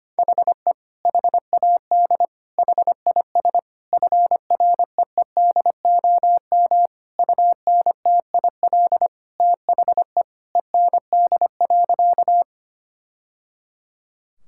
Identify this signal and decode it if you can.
Morse code, heard as a keyed tone, about 25 wpm, 700 Hz. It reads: HE HAD HIS FREEDOM UNTIL THE END.
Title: Sentences from Top 1000 Words - Review Part 5 - 25wpm